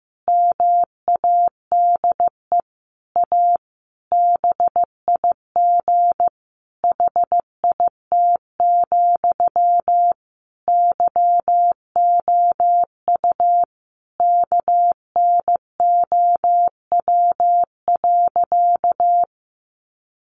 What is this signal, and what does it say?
Morse code, heard as a keyed tone, about 15 wpm, 700 Hz. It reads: MADE A BIG HIT, YOU KNOW.
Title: Sentences from Top 1200 Words 15wpm